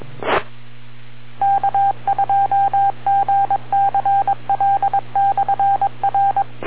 Morse code, heard as a keyed tone, about 22 wpm, 790 Hz. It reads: E K2GCL/R